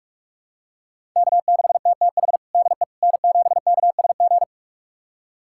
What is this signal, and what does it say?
Morse code, heard as a keyed tone, about 46 wpm, 700 Hz. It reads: K6TTH DE N7KSG
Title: USA Callsign DE Callsign - 5 Characters - 45wpm